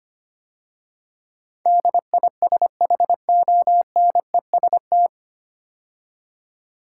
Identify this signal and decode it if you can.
Morse code, heard as a keyed tone, about 25 wpm, 700 Hz. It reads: DISHONEST
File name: practice / 25wpm / 9 Letter Words - First 3 Letters Given - 25wpm